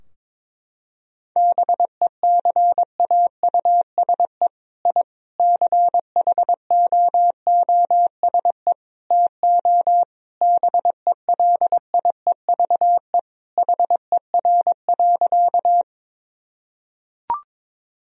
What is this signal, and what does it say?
Morse code, heard as a keyed tone, about 22 wpm, 700 Hz. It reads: BECAUSE I CHOOSE TO BELIEVE HER.
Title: Sentences from Top 1000 Words - Review Part 4 - 22wpm